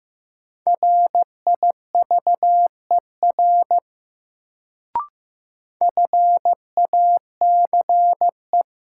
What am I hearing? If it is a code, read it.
Morse code, heard as a keyed tone, about 15 wpm, 700 Hz. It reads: RIVER FACE
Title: Sentences from Top 300 Words - New Word Drill - 15wpm